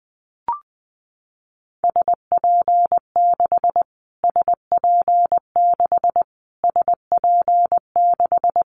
Morse code, heard as a keyed tone, about 20 wpm, 700 Hz. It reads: SP6 SP6 SP6
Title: Callsign Prefixes for Top 25 DXCC - Repeated 3x - 20wpm